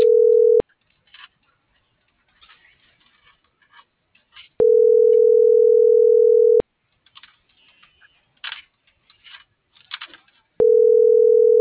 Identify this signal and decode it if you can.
Ringback tone (North America), 440 + 480 Hz.
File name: CENTRAL MULTIMODAL / out kimberly.suarez